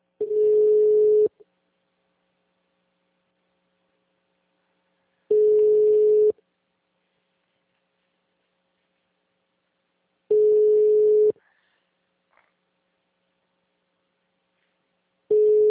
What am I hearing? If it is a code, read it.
Ringback tone (Europe, 425 Hz).